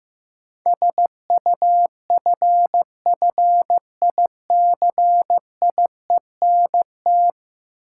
Morse code, heard as a keyed tone, about 15 wpm, 700 Hz. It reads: SUFFICIENT
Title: Sentences from Top 1400 Words - New Word Drill Alt - 15wpm